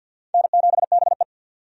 Morse code, heard as a keyed tone, about 52 wpm, 700 Hz. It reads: N7BE
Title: USA Call Signs - 4 Characters - 50wpm